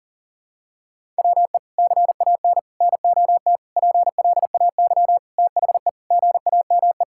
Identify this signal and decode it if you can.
Morse code, heard as a keyed tone, about 37 wpm, 700 Hz. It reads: WE CAN NOT PLAY THE GAME
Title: Sentences from Top 1000 Words - Review Part 6 - 40wpm